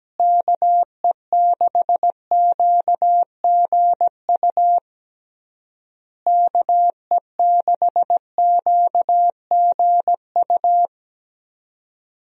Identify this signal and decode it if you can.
Morse code, heard as a keyed tone, about 17 wpm, 700 Hz. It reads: KE6QGU KE6QGU
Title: USA Call Signs - 6 Characters - Repeated 3x - 17wpm